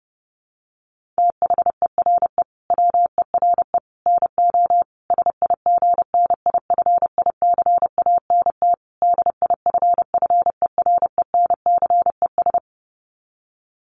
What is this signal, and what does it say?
Morse code, heard as a keyed tone, about 30 wpm, 700 Hz. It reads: THERE WERE NO SIGNIFICANT DIFFERENCES
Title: Sets of 5 Words Encore2 30wpm